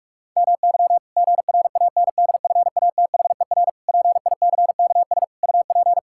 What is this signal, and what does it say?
Morse code, heard as a keyed tone, about 41 wpm, 700 Hz. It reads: MY GRANDFATHER PICKS UP
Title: Pangram Challenge v2 - 45wpm